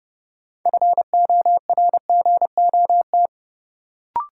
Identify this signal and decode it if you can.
Morse code, heard as a keyed tone, about 32 wpm, 700 Hz. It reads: FORGOT
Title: Sentences from Top 1400 Words - New Word Drill - 30wpm